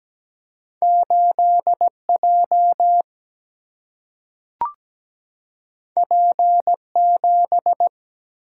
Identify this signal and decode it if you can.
Morse code, heard as a keyed tone, about 17 wpm, 700 Hz. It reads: 8J P7
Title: Single-Letter Single-Number 17wpm